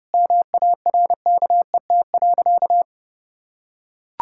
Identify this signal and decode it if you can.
Morse code, heard as a keyed tone, about 30 wpm, 700 Hz. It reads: MARKET.
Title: Sentences from Top 1000 Words - Review Part 6 - 30wpm